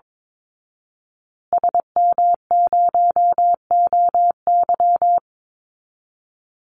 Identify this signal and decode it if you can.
Morse code, heard as a keyed tone, about 22 wpm, 700 Hz. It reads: SM0OY
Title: Call Signs Repeated 2x - 22wpm